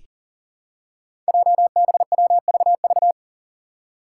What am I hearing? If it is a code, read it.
Morse code, heard as a keyed tone, about 41 wpm, 700 Hz. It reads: JBWVV
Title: Five Challenging Letter-Number Combinations 40wpm